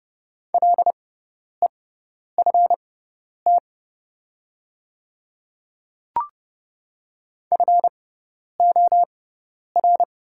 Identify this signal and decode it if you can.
Morse code, heard as a keyed tone, about 30 wpm, spaced as for 10 wpm, 700 Hz. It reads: LEFT FOR
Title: 6x Character Spacing - Top 500 Words - 30wpm